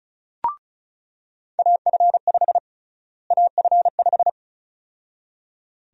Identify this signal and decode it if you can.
Morse code, heard as a keyed tone, about 36 wpm, 700 Hz. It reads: AF5 AF5
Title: USA Call Sign Prefixes - Repeated 2x - 35wpm